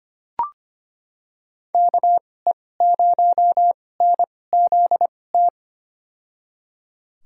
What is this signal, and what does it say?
Morse code, heard as a keyed tone, about 25 wpm, spaced as for 18 wpm, 700 Hz. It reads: KE0NZT
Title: USA Call Signs - 6 Characters - 2x Character Spacing - 25wpm